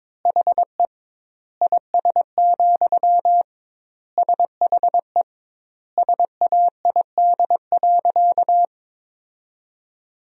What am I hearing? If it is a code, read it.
Morse code, heard as a keyed tone, about 21 wpm, 700 Hz. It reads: HE IS, SHE SAID.
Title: 2x Word-Spacing - Sentences from Top 100 Words - 22wpm